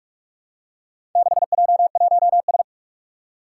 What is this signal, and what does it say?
Morse code, heard as a keyed tone, about 44 wpm, 700 Hz. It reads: 6J1S
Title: Four Challenging Letter-Number Combinations 45wpm